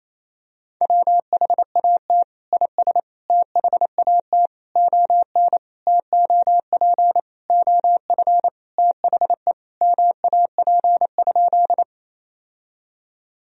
Morse code, heard as a keyed tone, about 28 wpm, 690 Hz. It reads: WHAT IS THAT ON TOP OF THE MAP?
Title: Sentences from Top 1000 Words - Review Part 2 - 28wpm